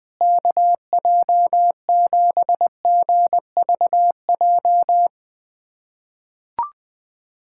Morse code, heard as a keyed tone, about 20 wpm, 700 Hz. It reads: KJ7GVJ
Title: USA Callsign DE Callsign 20wpm